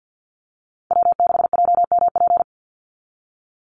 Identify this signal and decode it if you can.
Morse code, heard as a keyed tone, about 50 wpm, 700 Hz. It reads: W6JMP
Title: NA SOTA Chaser Callsigns 50wpm